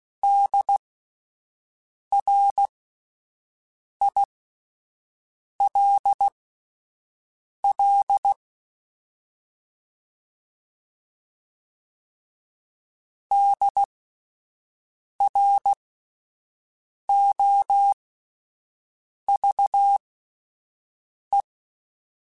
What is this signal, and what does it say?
Morse code, heard as a keyed tone, about 16 wpm, spaced as for 5 wpm, 800 Hz. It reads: DRILL DROVE